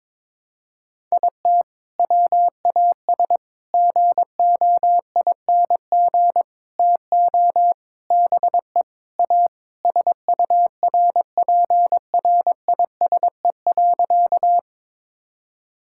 Morse code, heard as a keyed tone, about 22 wpm, 700 Hz. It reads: IT WAS GOING TO BE A SURPRISE.
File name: practice / 22wpm / Sentences with 4-Word Phrases - 22wpm